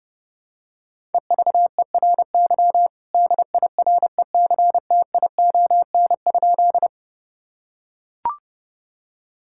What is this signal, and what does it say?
Morse code, heard as a keyed tone, about 32 wpm, 700 Hz. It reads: EVERY DIRECTION?
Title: Sentences from Top 1000 Words - Review Part 5 - 30wpm